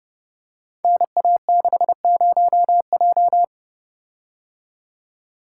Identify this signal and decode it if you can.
Morse code, heard as a keyed tone, about 30 wpm, 700 Hz. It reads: NA60J
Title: Code Groups - 5 Letter:Numbers - 30wpm